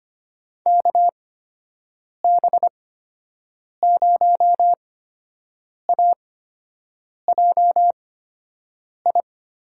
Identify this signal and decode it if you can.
Morse code, heard as a keyed tone, about 25 wpm, spaced as for 7 wpm, 700 Hz. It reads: KB0AJI
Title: USA Call Signs - 6 Characters - 8x Character Spacing - 25wpm